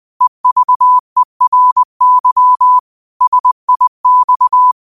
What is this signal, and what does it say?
Morse code, heard as a keyed tone, about 20 wpm, 1000 Hz. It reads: EVERY SIX
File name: qotdfiles / 20wpm